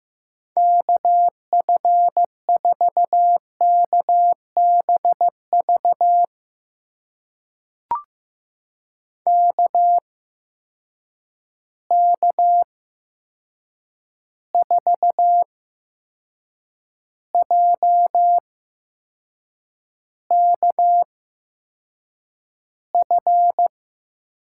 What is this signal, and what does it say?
Morse code, heard as a keyed tone, about 15 wpm, spaced as for 6 wpm, 700 Hz. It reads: KF4KBV KK4JKF
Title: USA Call Signs - 6 Characters - 8x Character Spacing - 15wpm